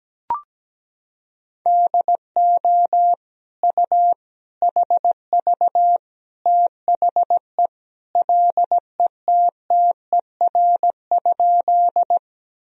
Morse code, heard as a keyed tone, about 17 wpm, 700 Hz. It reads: DO U HV THE LETTER?
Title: Sentences from Top 300 Words and Abbreviations 17wpm